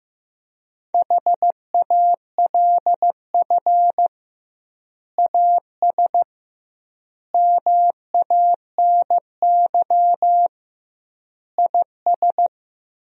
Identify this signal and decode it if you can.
Morse code, heard as a keyed tone, about 15 wpm, 700 Hz. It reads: HALF AS MANY IS
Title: 2x Word-Spacing - Sentences from Top 400 Words - 15wpm